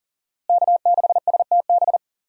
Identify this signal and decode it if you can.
Morse code, heard as a keyed tone, about 40 wpm, 700 Hz. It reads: K6STB
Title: USA Call Signs - 5 Characters - Repeated 3x - 40wpm